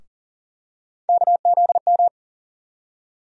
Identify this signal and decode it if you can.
Morse code, heard as a keyed tone, about 40 wpm, 700 Hz. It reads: KZM
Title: USA Call Sign Postfixes - 3 Letters 40wpm